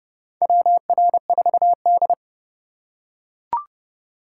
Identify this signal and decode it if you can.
Morse code, heard as a keyed tone, about 31 wpm, 700 Hz. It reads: WR4D
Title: USA Call Signs - 4 Characters - 2x Character Spacing - 30wpm